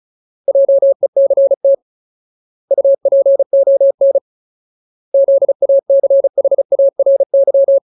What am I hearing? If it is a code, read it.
Morse code, heard as a keyed tone, about 34 wpm, 550 Hz. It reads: JECT UPON ZACHARY
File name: Ultimate Word List / 35WPM Ultimate